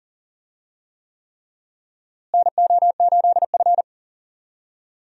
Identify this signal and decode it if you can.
Morse code, heard as a keyed tone, about 40 wpm, 700 Hz. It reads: NO8F
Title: USA Call Signs - 4 Characters - Repeated 3x - 40wpm